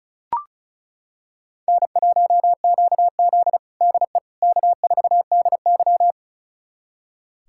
Morse code, heard as a keyed tone, about 36 wpm, 700 Hz. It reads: N1QZ DE K4DY
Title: USA Callsign DE Callsign - 4 Characters - 35wpm